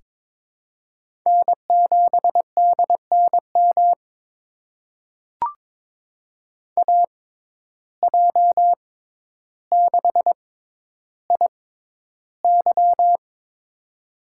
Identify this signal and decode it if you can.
Morse code, heard as a keyed tone, about 22 wpm, spaced as for 11 wpm, 700 Hz. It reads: N7DNM AJ6IY
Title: USA Call Signs - 5 Characters - 6x Character Spacing - 22wpm